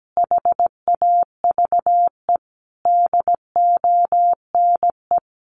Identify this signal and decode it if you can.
Morse code, heard as a keyed tone, about 17 wpm, 700 Hz. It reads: HAVE DONE
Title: Sentences from Top 1100 Words 17wpm